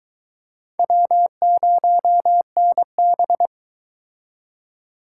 Morse code, heard as a keyed tone, about 23 wpm, 700 Hz. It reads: W0NB